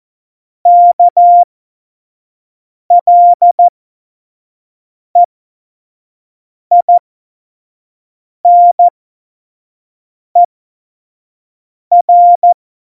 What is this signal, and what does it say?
Morse code, heard as a keyed tone, about 14 wpm, spaced as for 5 wpm, 700 Hz. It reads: KLEINER